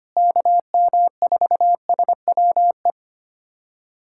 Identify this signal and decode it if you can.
Morse code, heard as a keyed tone, about 25 wpm, 700 Hz. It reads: KM4SWE
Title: USA Callsign DE Callsign 25wpm